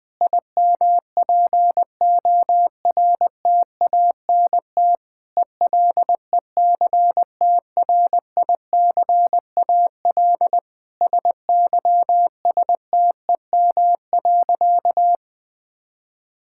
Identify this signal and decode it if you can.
Morse code, heard as a keyed tone, about 20 wpm, 700 Hz. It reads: IMPORTANT ELECTRICAL SYSTEM.
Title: Sentences from Top 1000 Words - Review Part 6 - 20wpm